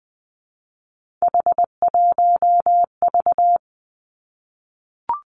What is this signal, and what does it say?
Morse code, heard as a keyed tone, about 20 wpm, 700 Hz. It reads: H1V
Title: Three Challenging Letter-Number Combinations 20wpm